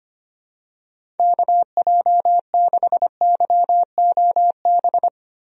Morse code, heard as a keyed tone, about 25 wpm, 700 Hz. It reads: KJ6YOB
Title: USA Call Signs - 6 Characters - Repeated 2x - 25wpm